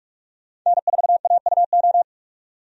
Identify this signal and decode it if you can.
Morse code, heard as a keyed tone, about 46 wpm, 700 Hz. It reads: N4AUO
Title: USA Call Signs - 5 Characters - 45wpm